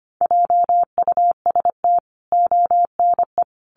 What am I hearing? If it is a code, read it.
Morse code, heard as a keyed tone, about 25 wpm, 700 Hz. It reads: JUST ONE
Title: Sentences from Top 1000 Words 25wpm